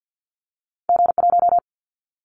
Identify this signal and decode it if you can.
Morse code, heard as a keyed tone, about 50 wpm, 700 Hz. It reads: G1
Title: Callsign Prefixes for Top 25 DXCC - 50wpm